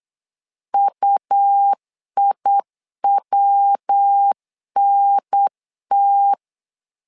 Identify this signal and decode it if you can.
Morse code, heard as a keyed tone, about 8 wpm, 800 Hz. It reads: UIWNT
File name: LEZIONI / 3 MWRO / 3 MW 8wpm 2501